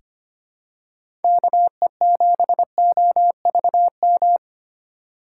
Morse code, heard as a keyed tone, about 25 wpm, 700 Hz. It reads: KE7OVM
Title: USA Call Signs - 6 Characters - 2x Character Spacing - 25wpm